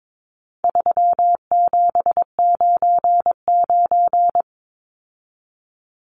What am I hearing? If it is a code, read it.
Morse code, heard as a keyed tone, about 22 wpm, 700 Hz. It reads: 3799
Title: Numbers - 4-Digit 22wpm